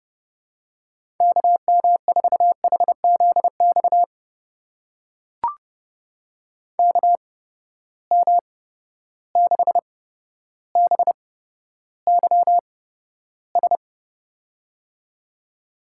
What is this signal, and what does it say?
Morse code, heard as a keyed tone, about 30 wpm, spaced as for 13 wpm, 700 Hz. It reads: KM4HZX KM6BYS
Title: USA Call Signs - 6 Characters - 8x Character Spacing - 30wpm